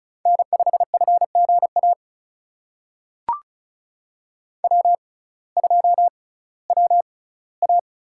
Morse code, heard as a keyed tone, about 34 wpm, spaced as for 19 wpm, 700 Hz. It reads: N5FGA W2WA
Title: USA Call Signs - 5 Characters - 6x Character Spacing - 35wpm